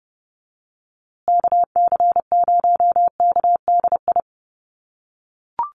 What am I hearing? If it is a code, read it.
Morse code, heard as a keyed tone, about 30 wpm, 700 Hz. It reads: KC0KDI
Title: USA Call Signs Repeated 3x 30wpm